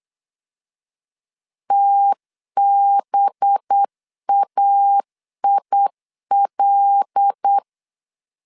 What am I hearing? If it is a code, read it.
Morse code, heard as a keyed tone, about 8 wpm, 800 Hz. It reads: TBAIL
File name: CW / DET / 1 6 1500C S B 8wpm 5142